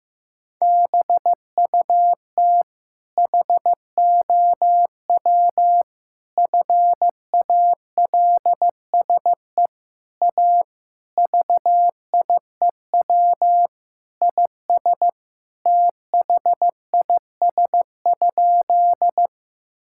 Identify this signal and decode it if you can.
Morse code, heard as a keyed tone, about 15 wpm, 700 Hz. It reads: BUT HOW FALSE A VIEW IS THIS?
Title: Sentences from Top 1600 Words 15wpm